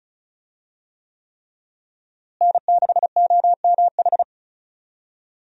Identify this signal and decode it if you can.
Morse code, heard as a keyed tone, about 34 wpm, 700 Hz. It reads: N6OMH
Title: USA Call Signs - 5 Characters - Repeated 3x - 35wpm